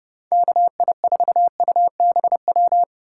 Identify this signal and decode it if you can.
Morse code, heard as a keyed tone, about 30 wpm, 700 Hz. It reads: KI4UBW